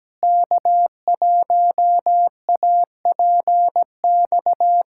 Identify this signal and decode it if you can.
Morse code, heard as a keyed tone, about 17 wpm, 700 Hz. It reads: K1APX
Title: USA Callsign DE Callsign - 5 Characters - 17wpm